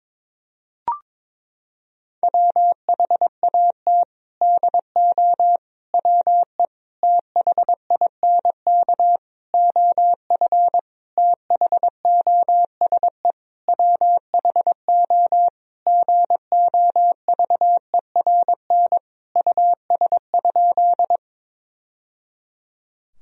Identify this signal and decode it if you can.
Morse code, heard as a keyed tone, about 22 wpm, 700 Hz. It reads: WHAT DO WE THINK OF THOSE WHO GOVERN US?
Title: Sentences from Top 1000 Words - Review Part 2 - 22wpm